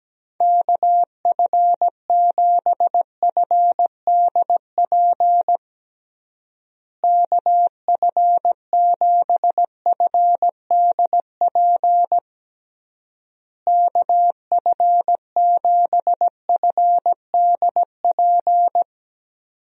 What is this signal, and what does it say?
Morse code, heard as a keyed tone, about 17 wpm, 700 Hz. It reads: KF7FDP KF7FDP KF7FDP
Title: USA Call Signs - 6 Characters - Repeated 3x - 17wpm